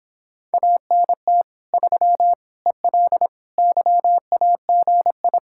Morse code, heard as a keyed tone, about 26 wpm, 700 Hz. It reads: ANT 3 EL YAGI